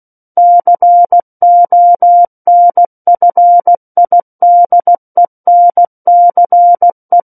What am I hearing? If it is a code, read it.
Morse code, heard as a keyed tone, about 16 wpm, 700 Hz. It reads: CONFIDENCE
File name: Book One / 07-Chapter 7 - 16 WPM